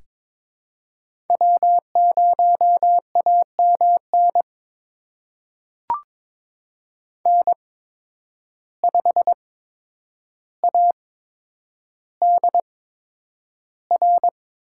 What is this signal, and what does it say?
Morse code, heard as a keyed tone, about 22 wpm, spaced as for 9 wpm, 700 Hz. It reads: W0AMN N5ADR
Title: USA Call Signs - 8x Character Spacing - 22wpm